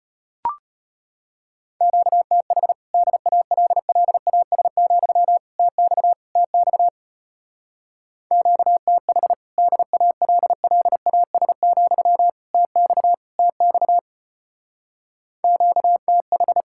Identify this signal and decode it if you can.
Morse code, heard as a keyed tone, about 33 wpm, 700 Hz. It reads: QTH DALLAS, TX TX QTH DALLAS, TX TX QTH